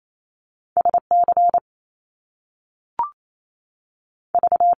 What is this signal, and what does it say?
Morse code, heard as a keyed tone, about 28 wpm, spaced as for 3 wpm, 700 Hz. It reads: S CV
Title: Top Two Letter Combinations in Words 28wpm